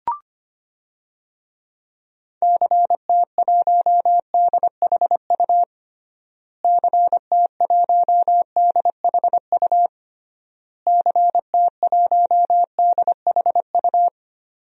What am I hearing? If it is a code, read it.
Morse code, heard as a keyed tone, about 25 wpm, 700 Hz. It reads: CT1DHU CT1DHU CT1DHU
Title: DXCC-100 Call Signs - 6 Characters - Repeated 3x - 25wpm